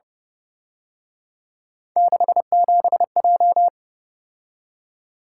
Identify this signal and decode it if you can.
Morse code, heard as a keyed tone, about 28 wpm, 700 Hz. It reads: 67J